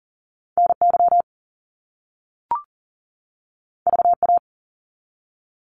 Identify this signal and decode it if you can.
Morse code, heard as a keyed tone, about 38 wpm, 700 Hz. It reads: NY VA
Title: US States 40wpm